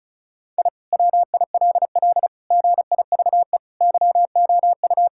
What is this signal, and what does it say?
Morse code, heard as a keyed tone, about 36 wpm, 700 Hz. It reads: I WILL GIVE YOU
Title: Sentences from Top 1000 Words - Review Part 6 - 35wpm